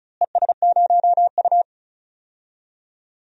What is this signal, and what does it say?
Morse code, heard as a keyed tone, about 34 wpm, 700 Hz. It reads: ES0U
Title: DXCC-100 Call Signs - 4 Characters - Repeated 3x - 35wpm